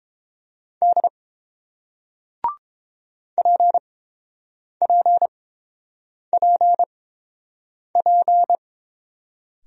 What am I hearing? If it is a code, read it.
Morse code, heard as a keyed tone, about 30 wpm, spaced as for 7 wpm, 700 Hz. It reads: D PPPP